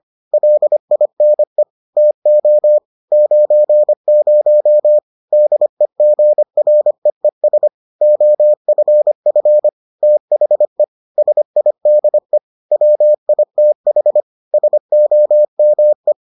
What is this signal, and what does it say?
Morse code, heard as a keyed tone, about 25 wpm, 600 Hz. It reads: LINE TO 90 DEGREES OFF THE SIDE WITH SOME